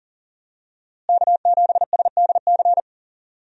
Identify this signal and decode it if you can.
Morse code, heard as a keyed tone, about 40 wpm, 700 Hz. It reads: K7SDC
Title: USA Call Signs - 5 Characters - Repeated 3x - 40wpm